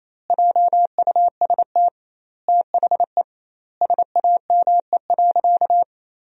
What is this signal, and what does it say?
Morse code, heard as a keyed tone, about 28 wpm, 700 Hz. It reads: JUST THE SAME.
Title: 2x Word-Spacing - Sentences from Top 300 Words - 28wpm